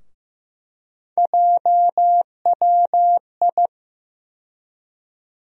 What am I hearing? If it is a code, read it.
Morse code, heard as a keyed tone, about 15 wpm, 700 Hz. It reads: JWI
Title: USA Call Sign Postfixes - 3 Letters 15wpm